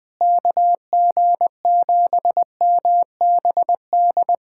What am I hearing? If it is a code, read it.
Morse code, heard as a keyed tone, about 20 wpm, 700 Hz. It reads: KG7MBD
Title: USA Callsign DE Callsign - 6 Characters - 20wpm